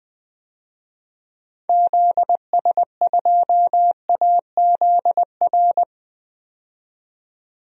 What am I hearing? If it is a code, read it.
Morse code, heard as a keyed tone, about 21 wpm, 700 Hz. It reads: ZS2AZR
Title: DXCC-100 Call Signs - 6 Characters - 20wpm